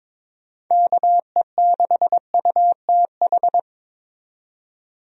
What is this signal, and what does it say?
Morse code, heard as a keyed tone, about 22 wpm, 700 Hz. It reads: KE6UTH
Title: USA Call Signs - 6 Characters - 22wpm